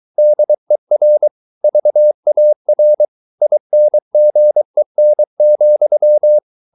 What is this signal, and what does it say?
Morse code, heard as a keyed tone, about 23 wpm, 600 Hz. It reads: DER VAR INGEN,
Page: Ferie 23 wpm